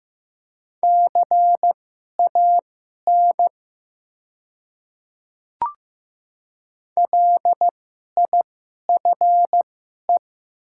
Morse code, heard as a keyed tone, about 15 wpm, spaced as for 11 wpm, 690 Hz. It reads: CAN LIFE
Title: 2x Character Spacing - Top 500 Words - 15wpm